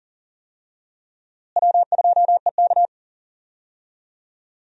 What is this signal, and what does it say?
Morse code, heard as a keyed tone, about 40 wpm, 700 Hz. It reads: W2EK
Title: USA Call Signs - 4 Characters - Repeated 2x - 40wpm